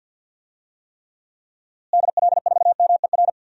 Morse code, heard as a keyed tone, about 52 wpm, 700 Hz. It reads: DL4MER
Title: DXCC-100 Call Signs - 6 Characters - Repeated 2x - 50wpm